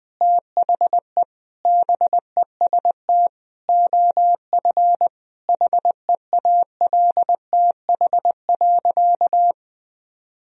Morse code, heard as a keyed tone, about 21 wpm, 700 Hz. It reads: THE BEST OF HEALTH.
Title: Sentences from Top 1000 Words - Review Part 3 - 20wpm